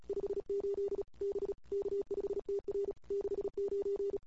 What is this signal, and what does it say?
Morse code, heard as a keyed tone, about 31 wpm, 410 Hz. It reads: 58 B K5TR 69
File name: CW / 2008-ss-cw-k5tr-op-n5rz -21:00